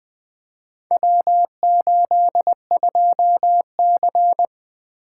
Morse code, heard as a keyed tone, about 20 wpm, 700 Hz. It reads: W82C
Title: Four Letter-Number Combinations - 20wpm